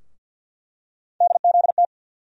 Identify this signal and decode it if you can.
Morse code, heard as a keyed tone, about 49 wpm, 700 Hz. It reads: DZT
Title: USA Call Sign Postfixes - 3 Letters 50wpm